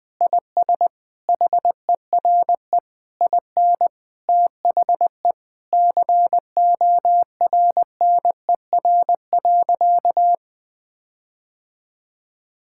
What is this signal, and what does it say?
Morse code, heard as a keyed tone, about 21 wpm, 700 Hz. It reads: IS HERE IN THE CORNER.
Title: Sentences from Top 1600 Words 20wpm